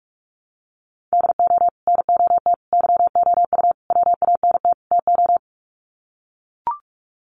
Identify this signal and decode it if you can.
Morse code, heard as a keyed tone, about 44 wpm, 700 Hz. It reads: DO NOT YOU WANT TO
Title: Sets of 5 Words Encore2 45wpm